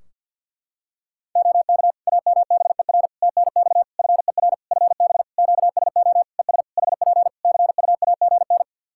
Morse code, heard as a keyed tone, about 51 wpm, 700 Hz. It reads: OK AMBER TNX FER FB QSO ES HP CUAGN
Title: Practice QSO - Exchange 4 - Ending - 50wpm